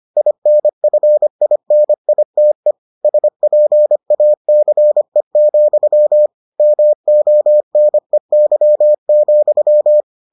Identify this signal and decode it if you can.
Morse code, heard as a keyed tone, about 25 wpm, 600 Hz. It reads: INFINITE SPACE, MONEY,